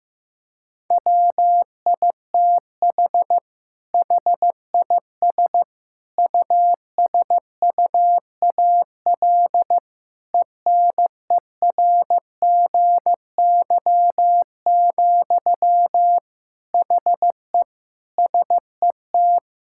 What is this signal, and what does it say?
Morse code, heard as a keyed tone, about 15 wpm, 700 Hz. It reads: WITH HIS USUAL ENERGY, HE SET
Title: Sentences from Top 1000 Words - Review Part 6 - 15wpm